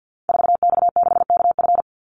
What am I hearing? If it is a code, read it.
Morse code, heard as a keyed tone, about 50 wpm, 700 Hz. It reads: 4X6KF
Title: Call Signs Repeated 2x - 50wpm